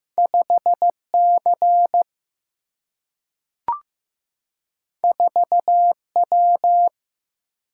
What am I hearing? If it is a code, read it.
Morse code, heard as a keyed tone, about 15 wpm, 700 Hz. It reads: 5 C 4 W
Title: Single-Letter Single-Number 15wpm